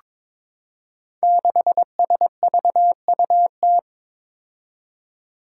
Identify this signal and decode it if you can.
Morse code, heard as a keyed tone, about 22 wpm, 700 Hz. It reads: 6SVUT
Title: Five Challenging Letter-Number Combinations 22wpm